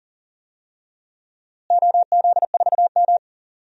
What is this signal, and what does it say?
Morse code, heard as a keyed tone, about 40 wpm, 700 Hz. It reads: OZ4M
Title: DXCC-100 Call Signs - 4 Characters - Repeated 2x - 40wpm